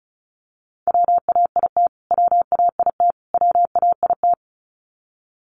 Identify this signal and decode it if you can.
Morse code, heard as a keyed tone, about 34 wpm, 700 Hz. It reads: WAIT WAIT WAIT
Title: Top 500 Words Repeated 35wpm